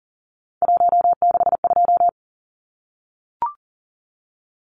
Morse code, heard as a keyed tone, about 40 wpm, 700 Hz. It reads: 162
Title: Numbers - 3-Digit 40wpm